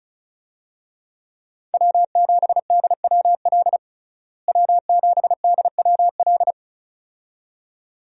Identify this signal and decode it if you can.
Morse code, heard as a keyed tone, about 34 wpm, 700 Hz. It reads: W7DWL W7DWL
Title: USA Call Signs - 5 Characters - Repeated 2x - 35wpm